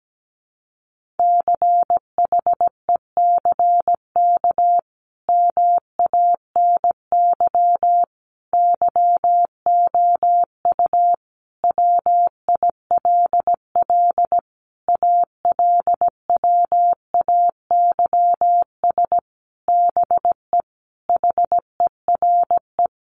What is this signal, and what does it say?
Morse code, heard as a keyed tone, about 17 wpm, 700 Hz. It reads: CHECK MANY YOU WILL ALWAYS BE HERE
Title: Sentences from Top 1500 Words 17wpm